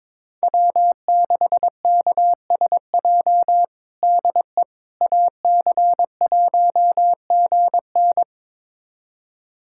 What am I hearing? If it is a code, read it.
Morse code, heard as a keyed tone, about 21 wpm, 700 Hz. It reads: W6KSJ DE AC1GN
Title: USA Callsign DE Callsign - 5 Characters - 22wpm